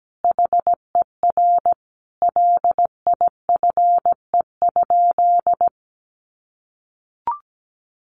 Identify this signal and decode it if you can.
Morse code, heard as a keyed tone, about 17 wpm, 700 Hz. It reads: HER LIFE?
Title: Sentences from Top 1500 Words 17wpm